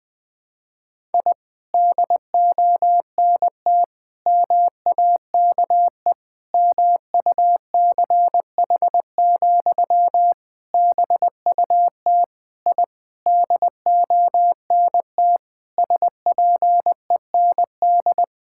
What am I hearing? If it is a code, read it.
Morse code, heard as a keyed tone, about 20 wpm, 700 Hz. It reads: I DONT MAKE MUCH, BUT I DONT SPEND
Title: Sentences from Top 1000 Words - Review Part 5 - 20wpm